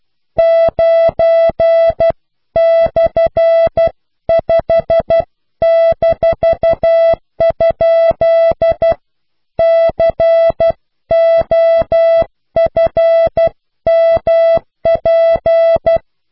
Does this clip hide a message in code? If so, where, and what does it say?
Yes: Morse code from 0.4 to 16.0 s, heard as a keyed tone, about 12 wpm, 660 Hz. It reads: 9/5-? COFMP